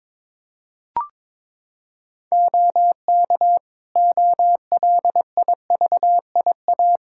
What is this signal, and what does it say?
Morse code, heard as a keyed tone, about 22 wpm, 700 Hz. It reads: OK OLIVIA
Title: Practice QSO - Exchange 2 - Info - 22wpm